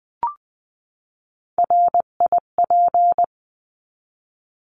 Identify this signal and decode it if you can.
Morse code, heard as a keyed tone, about 20 wpm, 700 Hz. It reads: RIP
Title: 259 Lesson 42d - Words 18-20wpm